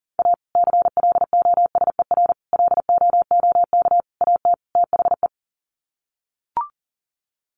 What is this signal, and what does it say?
Morse code, heard as a keyed tone, about 40 wpm, 700 Hz. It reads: A CLOSER LOOK AT THE
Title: Sets of 5 Words Encore 40wpm